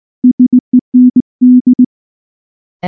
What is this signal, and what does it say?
Morse code, heard as a keyed tone, about 19 wpm, 250 Hz. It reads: SEND
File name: morse / morse-training-20WPM-3